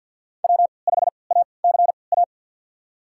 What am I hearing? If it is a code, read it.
Morse code, heard as a keyed tone, about 53 wpm, spaced as for 29 wpm, 700 Hz. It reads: W5ACA
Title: USA Call Signs - 3x Character Spacing - 50wpm